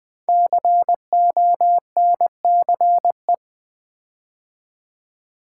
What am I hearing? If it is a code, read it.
Morse code, heard as a keyed tone, about 20 wpm, 700 Hz. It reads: CONCE
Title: Build A Word - 5 Letters At A Time - 10-Letter Words - 20wpm